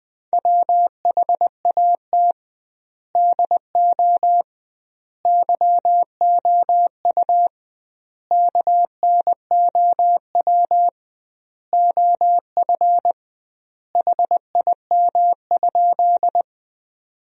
Morse code, heard as a keyed tone, about 21 wpm, 700 Hz. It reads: WHAT DO YOU KNOW OF HIM?
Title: 2x Word-Spacing - Sentences from Top 100 Words - 20wpm